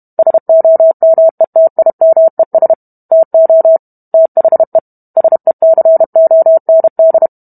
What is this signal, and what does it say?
Morse code, heard as a keyed tone, about 32 wpm, 650 Hz. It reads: SOMETIMES TO THE SECOND